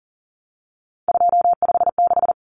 Morse code, heard as a keyed tone, about 40 wpm, 700 Hz. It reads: 256
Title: Common Cut Numbers - 3-Digit 40wpm